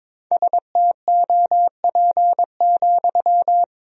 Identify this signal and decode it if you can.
Morse code, heard as a keyed tone, about 22 wpm, 700 Hz. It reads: STOP,
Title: 4x Word-Spacing - Sentences from Top 300 Words - 22wpm